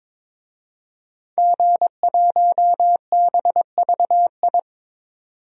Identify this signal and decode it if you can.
Morse code, heard as a keyed tone, about 22 wpm, 700 Hz. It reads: G1BVI
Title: DXCC-100 Call Signs - 5 Characters - 22wpm